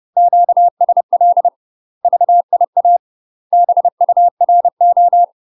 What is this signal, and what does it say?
Morse code, heard as a keyed tone, about 30 wpm, 700 Hz. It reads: QSL VIA BURO
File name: UK MASTER QSOs TRAINING / 30WPM / UK MASTER QSOs 0039